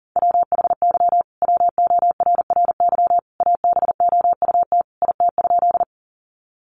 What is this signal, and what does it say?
Morse code, heard as a keyed tone, about 40 wpm, 700 Hz. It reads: WHY WORRY ABOUT IT?
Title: Sentences from Top 1100 Words 40wpm